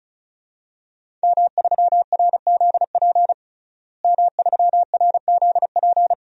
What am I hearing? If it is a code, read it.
Morse code, heard as a keyed tone, about 36 wpm, 700 Hz. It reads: M3RZP M3RZP
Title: DXCC-100 Call Signs - 5 Characters - Repeated 2x - 35wpm